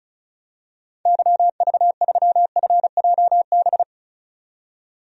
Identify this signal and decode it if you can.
Morse code, heard as a keyed tone, about 34 wpm, 700 Hz. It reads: YV3FJB
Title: DXCC-100 Call Signs - 6 Characters - 35wpm